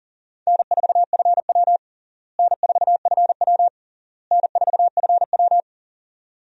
Morse code, heard as a keyed tone, about 40 wpm, 700 Hz. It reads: N4FW N4FW N4FW
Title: USA Call Signs - 4 Characters - Repeated 3x - 40wpm